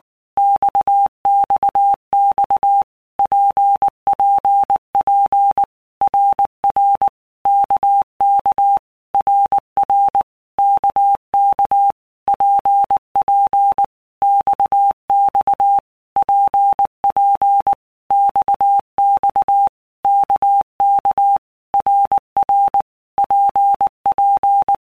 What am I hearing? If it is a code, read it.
Morse code, heard as a keyed tone, about 19 wpm, 800 Hz. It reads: XXX PPP RR KK RR KK PP XX PP XX KK RR PP
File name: All Groups 1-minute Blocks / Groups@18 / RKPX